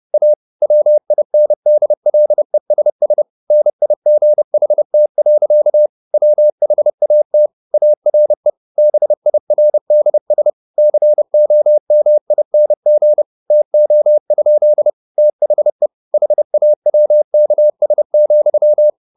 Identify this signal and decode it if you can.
Morse code, heard as a keyed tone, about 30 wpm, 600 Hz. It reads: A WINDLESS NIGHT. WHAT ARE BIRDS COMING TO? THE HAWKS,
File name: the old man and the sea / the old man and the sea 1 30wpm